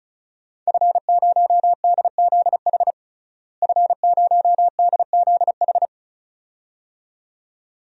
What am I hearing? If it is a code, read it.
Morse code, heard as a keyed tone, about 36 wpm, 700 Hz. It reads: F0DZH F0DZH
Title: DXCC-100 Call Signs - 5 Characters - Repeated 2x - 35wpm